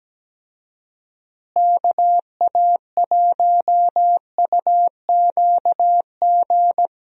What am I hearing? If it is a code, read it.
Morse code, heard as a keyed tone, about 17 wpm, 700 Hz. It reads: KA1UQG
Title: USA Call Signs - 6 Characters - Repeated 3x - 17wpm